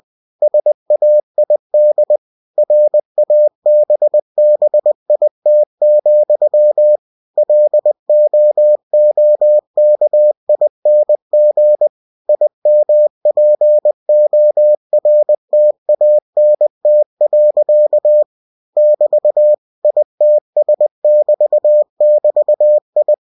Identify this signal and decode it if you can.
Morse code, heard as a keyed tone, about 20 wpm, 600 Hz. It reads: SAID RABBIT, LOOKING IMPORTANT. <BT> ITS<BT><BT>I
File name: cw-books / winnie-the-pooh / winnie-20wpm-page-0084